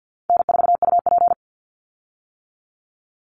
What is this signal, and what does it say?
Morse code, heard as a keyed tone, about 50 wpm, 700 Hz. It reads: N4UP
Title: Easy Call Signs - 50wpm